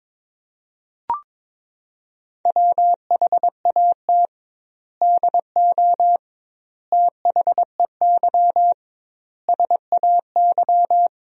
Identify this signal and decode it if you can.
Morse code, heard as a keyed tone, about 22 wpm, 700 Hz. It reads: WHAT DO THEY SAY
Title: 2x Word-Spacing - Sentences from Top 200 Words - 22wpm